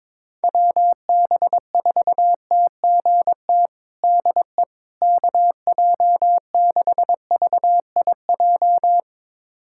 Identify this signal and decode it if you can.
Morse code, heard as a keyed tone, about 22 wpm, 700 Hz. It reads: WB4TGT DE KJ6VIJ
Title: USA Callsign DE Callsign - 6 Characters - 22wpm